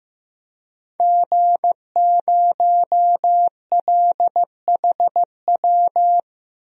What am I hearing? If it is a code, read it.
Morse code, heard as a keyed tone, about 15 wpm, 700 Hz. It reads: G0LHW
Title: DXCC-100 Call Signs - 5 Characters - 15wpm